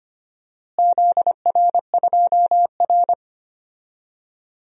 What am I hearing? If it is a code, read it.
Morse code, heard as a keyed tone, about 25 wpm, 700 Hz. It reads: ZR2R
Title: DXCC-100 Call Signs - 4 Characters - Repeated 3x - 25wpm